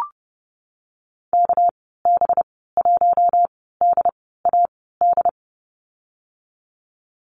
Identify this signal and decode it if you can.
Morse code, heard as a keyed tone, about 30 wpm, spaced as for 17 wpm, 700 Hz. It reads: KB1DAD
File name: Lesson 37g - Letter x - Call Signs 30wpm